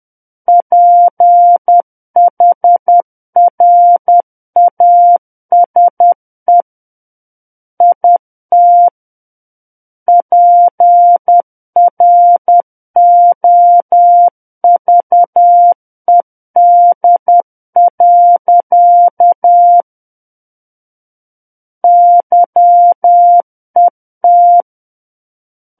Morse code, heard as a keyed tone, about 10 wpm, 700 Hz. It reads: PHRASE IT PROVED. YET
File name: Book One / 01-Chapter 1 - 10 WPM